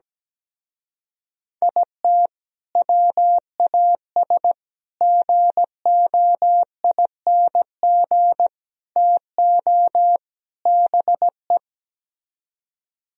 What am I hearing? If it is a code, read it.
Morse code, heard as a keyed tone, about 17 wpm, 700 Hz. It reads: IT WAS GOING TO BE
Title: Speed-Racing - Sets of 5 Words 17wpm